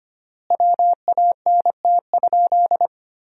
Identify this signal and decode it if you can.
Morse code, heard as a keyed tone, about 25 wpm, 700 Hz. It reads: WANT?
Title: 3x Word-Spacing - Sentences from Top 400 Words - 25wpm